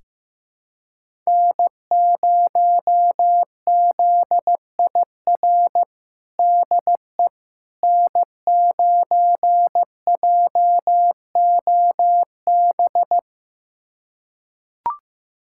Morse code, heard as a keyed tone, about 15 wpm, 700 Hz. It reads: N0ZIR DE N9JOB
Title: USA Callsign DE Callsign - 5 Characters - 15wpm